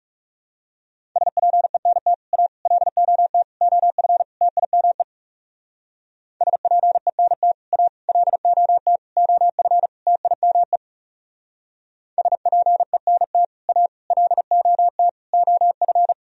Morse code, heard as a keyed tone, about 35 wpm, 700 Hz. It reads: SPENT A LOT OF TIME SPENT A LOT OF TIME SPENT A LOT OF
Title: Speed-Racing - Sets of 5 Words 30wpm